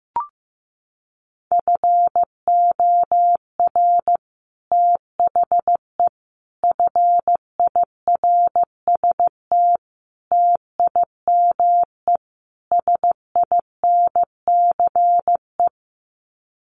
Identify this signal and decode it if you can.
Morse code, heard as a keyed tone, about 15 wpm, 700 Hz. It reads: FOR THE FIRST TIME SINCE
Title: Sets of 5 Words 15wpm